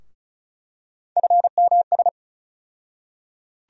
Morse code, heard as a keyed tone, about 34 wpm, 700 Hz. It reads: FMS
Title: USA Call Sign Postfixes - 3 Letters 35wpm